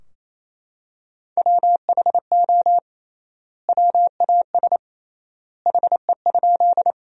Morse code, heard as a keyed tone, about 28 wpm, 700 Hz. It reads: WHO WAS HE?
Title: 3x Word-Spacing - Sentences from Top 100 Words - 28wpm